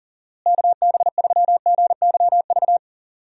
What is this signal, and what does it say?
Morse code, heard as a keyed tone, about 40 wpm, 700 Hz. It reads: KB3GYV
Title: USA Call Signs - 6 Characters - Repeated 2x - 40wpm